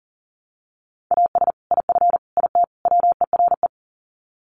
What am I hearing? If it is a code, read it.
Morse code, heard as a keyed tone, about 40 wpm, 700 Hz. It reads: AS IF IT WERE
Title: Sets of 4 Words 40wpm